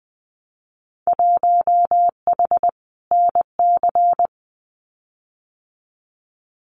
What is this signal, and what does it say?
Morse code, heard as a keyed tone, about 20 wpm, 700 Hz. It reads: 1H NC
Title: Winter Field Day Exchanges Encore 20wpm